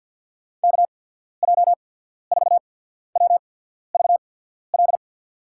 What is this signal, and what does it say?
Morse code, heard as a keyed tone, about 51 wpm, spaced as for 14 wpm, 700 Hz. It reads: KJ4WVL
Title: USA Call Signs - 6 Characters - 8x Character Spacing - 50wpm